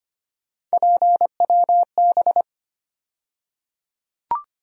Morse code, heard as a keyed tone, about 25 wpm, 700 Hz. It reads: PWB
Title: USA Call Sign Postfixes - 3 Letters 25wpm